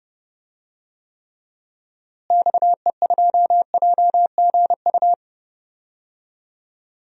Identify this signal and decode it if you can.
Morse code, heard as a keyed tone, about 29 wpm, 700 Hz. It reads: XE2JGU
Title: DXCC-100 Call Signs - 6 Characters - Repeated 2x - 30wpm